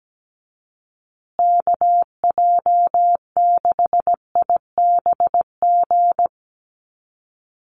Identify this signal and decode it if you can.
Morse code, heard as a keyed tone, about 17 wpm, 700 Hz. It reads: KJ6IBG
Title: USA Call Signs Repeated 2x 17wpm